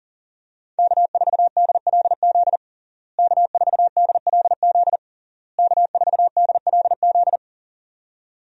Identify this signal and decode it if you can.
Morse code, heard as a keyed tone, about 40 wpm, 700 Hz. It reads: K4DLZ K4DLZ K4DLZ
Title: USA Call Signs - 5 Characters - Repeated 3x - 40wpm